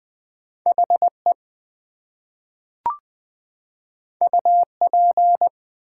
Morse code, heard as a keyed tone, about 20 wpm, 700 Hz. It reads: HE UP
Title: Two Letter Words - 20wpm